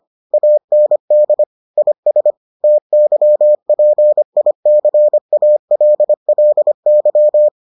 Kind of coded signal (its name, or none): Morse code